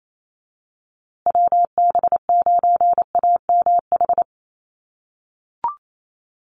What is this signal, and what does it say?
Morse code, heard as a keyed tone, about 28 wpm, 700 Hz. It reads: WB9AMH
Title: USA Call Signs Repeated 3x 28wpm